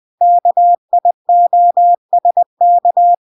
Morse code, heard as a keyed tone, about 20 wpm, 700 Hz. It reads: KIOSK